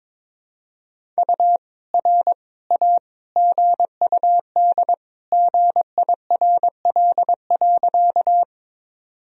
Morse code, heard as a keyed tone, about 22 wpm, 700 Hz. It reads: U R A GUD GIRL.
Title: Sentences from Top 400 Words and Abbreviations 22wpm